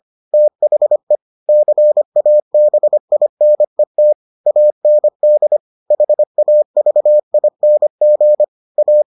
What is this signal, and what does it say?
Morse code, heard as a keyed tone, about 25 wpm, 600 Hz. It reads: THE CABINET AND HAVING A